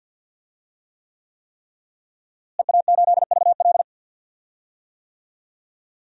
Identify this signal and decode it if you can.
Morse code, heard as a keyed tone, about 46 wpm, 700 Hz. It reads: EA8VL
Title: DXCC-100 Call Signs - 5 Characters - Repeated 3x - 50wpm